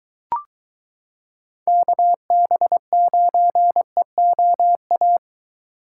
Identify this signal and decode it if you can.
Morse code, heard as a keyed tone, about 23 wpm, 700 Hz. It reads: KB9EOA